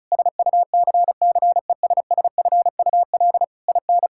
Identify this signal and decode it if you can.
Morse code, heard as a keyed tone, about 36 wpm, 700 Hz. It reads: SUCCESSFUL IN
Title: Sentences from Top 1000 Words - Review Part 6 - 35wpm